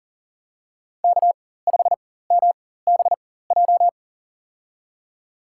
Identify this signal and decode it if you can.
Morse code, heard as a keyed tone, about 40 wpm, spaced as for 19 wpm, 700 Hz. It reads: K5MBJ
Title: USA Call Signs - 5 Characters - 4x Character Spacing - 40wpm